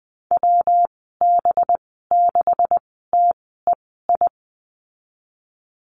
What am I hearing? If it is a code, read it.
Morse code, heard as a keyed tone, about 20 wpm, spaced as for 14 wpm, 700 Hz. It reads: WB6TEI
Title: 231 Lesson 37g - Letter x - Call Signs 20wpm